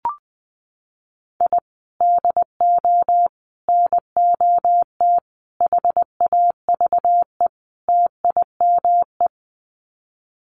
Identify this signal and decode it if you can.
Morse code, heard as a keyed tone, about 20 wpm, 700 Hz. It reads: I DO NOT HAVE TIME
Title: Sets of 5 Words Encore 20wpm